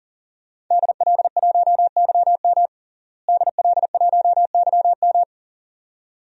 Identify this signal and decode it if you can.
Morse code, heard as a keyed tone, about 40 wpm, 700 Hz. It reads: DL1YM DL1YM
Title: DXCC-100 Call Signs - 5 Characters - Repeated 2x - 40wpm